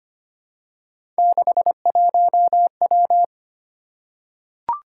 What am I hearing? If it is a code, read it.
Morse code, heard as a keyed tone, about 25 wpm, 700 Hz. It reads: 61W
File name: Three Letter-Number Combinations - 25wpm